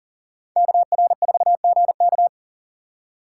Morse code, heard as a keyed tone, about 37 wpm, 700 Hz. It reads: KR4GK
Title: USA Call Signs - 5 Characters - 2x Character Spacing - 40wpm